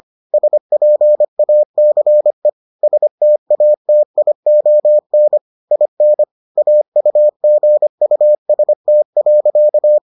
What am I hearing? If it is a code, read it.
Morse code, heard as a keyed tone, about 25 wpm, 600 Hz. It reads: SPACE STATION IN AUGUST.